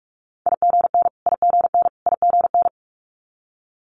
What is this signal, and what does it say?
Morse code, heard as a keyed tone, about 46 wpm, 700 Hz. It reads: IGN IGN IGN
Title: Multi-letter Phonemes Repeated 3x 45wpm